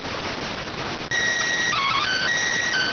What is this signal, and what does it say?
SSTV (slow-scan television), PD 290.